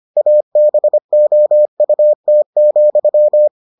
Morse code, heard as a keyed tone, about 25 wpm, 600 Hz. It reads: ABOUT,